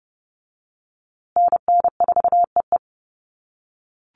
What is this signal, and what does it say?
Morse code, heard as a keyed tone, about 30 wpm, 700 Hz. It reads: NN4EE
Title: USA Call Signs Repeated 2x 30wpm